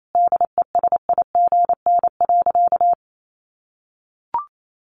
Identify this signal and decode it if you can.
Morse code, heard as a keyed tone, about 28 wpm, 700 Hz. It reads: DESIGN.
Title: Sentences from Top 800 Words 28wpm